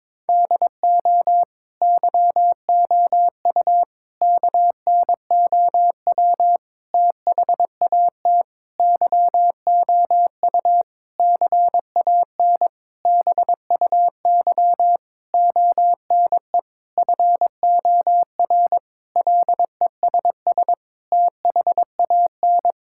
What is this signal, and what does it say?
Morse code, heard as a keyed tone, about 21 wpm, 700 Hz. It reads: DO YOU KNOW THAT YOU CAN BUY ONE FOR LESS THAN
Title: Sentences from Top 1000 Words - Review Part 5 - 22wpm